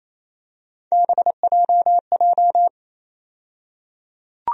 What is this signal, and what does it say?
Morse code, heard as a keyed tone, about 28 wpm, 700 Hz. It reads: BJJ
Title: USA Call Sign Postfixes - 3 Letters 28wpm